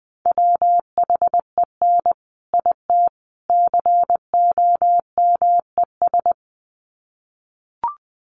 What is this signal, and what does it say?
Morse code, heard as a keyed tone, about 19 wpm, 700 Hz. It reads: WHEN IT COMES
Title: Sets of 3 Words 20wpm